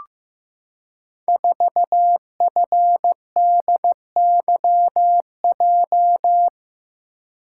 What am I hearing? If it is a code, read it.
Morse code, heard as a keyed tone, about 15 wpm, 700 Hz. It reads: 4FDYJ
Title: Code Groups - 5 Letter:Numbers - 15wpm